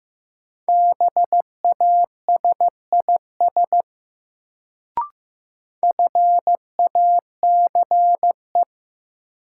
Morse code, heard as a keyed tone, about 16 wpm, 700 Hz. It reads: BASIS FACE
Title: Sets of 3 Words - New Word Drill - 15wpm